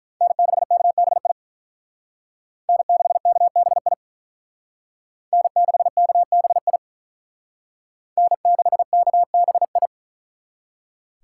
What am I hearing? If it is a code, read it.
Morse code, heard as a keyed tone, about 43 wpm, 700 Hz. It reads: N6KBI N6KBI N6KBI N6KBI
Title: Speed Racing - USA 1x3 Call Signs 35wpm